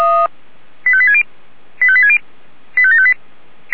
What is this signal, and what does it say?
Five-tone selective call (ZVEI), heard as quick runs of five tones, tones of about 70 ms. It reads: 85860 85860 85758
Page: ZVEI